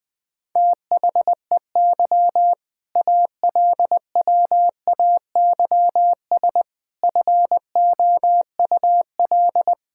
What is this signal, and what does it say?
Morse code, heard as a keyed tone, about 20 wpm, 700 Hz. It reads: THEY ALWAYS FOUL
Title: All-Age Appropriate Jokes - 20wpm